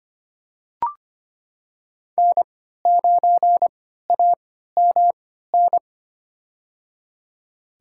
Morse code, heard as a keyed tone, about 25 wpm, spaced as for 14 wpm, 700 Hz. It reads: N9AMN
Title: USA Call Signs - 5 Characters - 3x Character Spacing - 25wpm